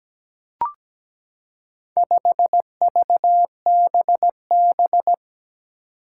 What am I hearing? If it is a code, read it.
Morse code, heard as a keyed tone, about 17 wpm, 700 Hz. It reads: 5VBB
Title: Four Challenging Letter-Number Combinations 17wpm